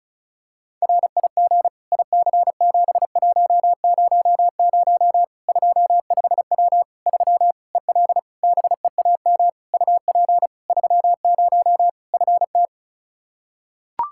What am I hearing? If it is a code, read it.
Morse code, heard as a keyed tone, about 36 wpm, 700 Hz. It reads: RIG IC7100 25W 3 EL BEAM UP 30 FT
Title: Practice QSO - Exchange 2 - Info - 35wpm